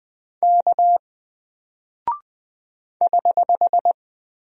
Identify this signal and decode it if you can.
Morse code, heard as a keyed tone, about 20 wpm, 700 Hz. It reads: K <HH>
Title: Single Character (Letter:Number:Punctuation:Prosign) Alt 20wpm